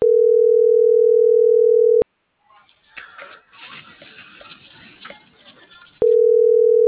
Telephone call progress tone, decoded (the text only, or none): ringback tone